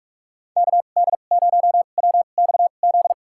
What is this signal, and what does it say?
Morse code, heard as a keyed tone, about 46 wpm, spaced as for 32 wpm, 700 Hz. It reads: KD0WXZ
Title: USA Call Signs - 6 Characters - 2x Character Spacing - 45wpm